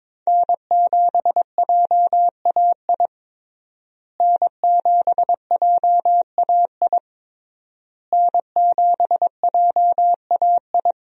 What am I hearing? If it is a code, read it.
Morse code, heard as a keyed tone, about 22 wpm, 700 Hz. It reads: N7JAI N7JAI N7JAI
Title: USA Call Signs - 5 Characters - Repeated 3x - 22wpm